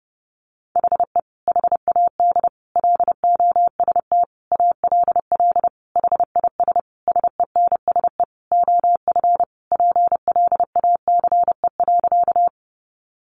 Morse code, heard as a keyed tone, about 30 wpm, 700 Hz. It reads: HE HAD LOST ALL HIS SENSE OF PLACE.
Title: Sentences from Top 1400 Words 30wpm